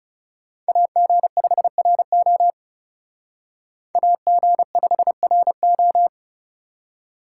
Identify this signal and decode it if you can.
Morse code, heard as a keyed tone, about 36 wpm, 700 Hz. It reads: AG5RO AG5RO
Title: Speed Racing - USA Call Signs 30wpm